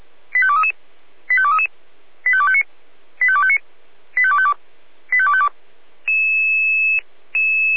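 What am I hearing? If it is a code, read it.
Five-tone selective call (ZVEI), heard as quick runs of five tones, tones of about 80 ms. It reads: 85233 85233 85259 85259 85252 85252
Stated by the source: ZVEI five-tone selective call received on an SDR.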